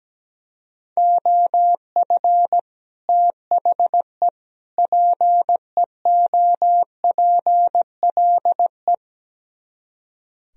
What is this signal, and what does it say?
Morse code, heard as a keyed tone, about 17 wpm, 700 Hz. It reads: OF THE PEOPLE
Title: Speed-Racing - Sets of 3 Words 17wpm